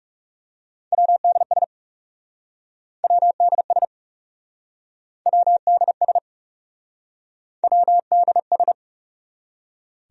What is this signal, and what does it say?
Morse code, heard as a keyed tone, about 40 wpm, 700 Hz. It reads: WDS WDS WDS WDS